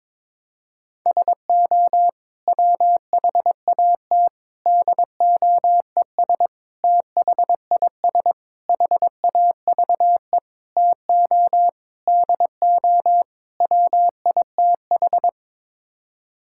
Morse code, heard as a keyed tone, about 22 wpm, 700 Hz. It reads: SO WHAT DOES THIS HAVE TO DO WITH
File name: practice / 22wpm / Sets of 8 Words - 22wpm